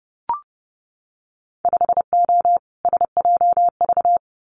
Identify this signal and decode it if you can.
Morse code, heard as a keyed tone, about 30 wpm, 700 Hz. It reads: 5O SJV
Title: Winter Field Day Exchanges Encore 30wpm